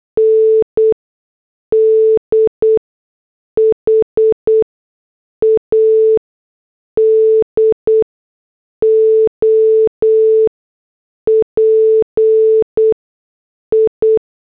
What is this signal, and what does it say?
Morse code, heard as a keyed tone, about 8 wpm, spaced as for 6 wpm, 440 Hz. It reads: ND HAD OPI